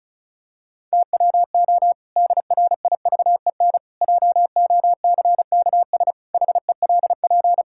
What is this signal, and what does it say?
Morse code, heard as a keyed tone, about 36 wpm, 700 Hz. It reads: TWO DRIVEN JOCKS HELP
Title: Pangram Challenge v2 - 35wpm